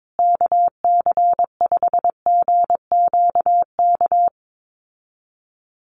Morse code, heard as a keyed tone, about 22 wpm, 700 Hz. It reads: KC5GQK
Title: USA Call Signs Repeated 2x 22wpm